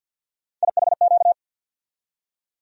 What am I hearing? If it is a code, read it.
Morse code, heard as a keyed tone, about 46 wpm, 700 Hz. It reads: IHQ
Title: USA Call Sign Postfixes - 3 Letters 50wpm